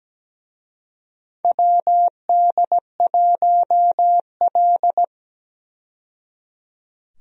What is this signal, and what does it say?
Morse code, heard as a keyed tone, about 17 wpm, 700 Hz. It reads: WD1L
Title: USA Call Signs - 4 Characters - 17wpm